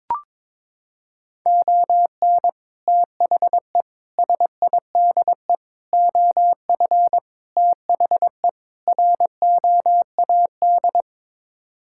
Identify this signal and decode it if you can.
Morse code, heard as a keyed tone, about 22 wpm, 700 Hz. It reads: ON THE SIDE OF THE ROAD
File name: practice / 22wpm / Sets of 6 Words - 22wpm